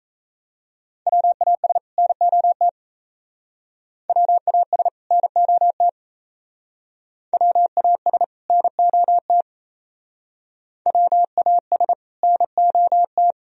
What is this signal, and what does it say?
Morse code, heard as a keyed tone, about 35 wpm, 700 Hz. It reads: WAS NOT WAS NOT WAS NOT WAS NOT
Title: Speed-Racing - Sets of 2 Words 28wpm